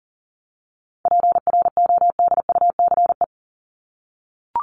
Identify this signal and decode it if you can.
Morse code, heard as a keyed tone, about 40 wpm, 700 Hz. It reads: PRODUCE
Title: Top 500 Words 40wpm